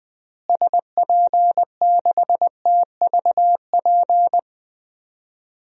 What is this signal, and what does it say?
Morse code, heard as a keyed tone, about 20 wpm, 700 Hz. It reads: SP6TVP
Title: DXCC-100 Call Signs - 6 Characters - Repeated 2x - 20wpm